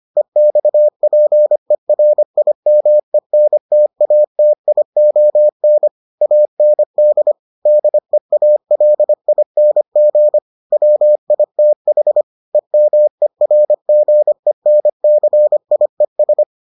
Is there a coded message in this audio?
Yes: Morse code from 0.2 to 16.4 s, heard as a keyed tone, about 25 wpm, 600 Hz. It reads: EXPERIMENTATION AND DEALING WITH EMERGENCIES